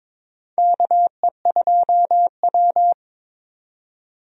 Morse code, heard as a keyed tone, about 22 wpm, 700 Hz. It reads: KE2W
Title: USA Call Signs - 4 Characters - 22wpm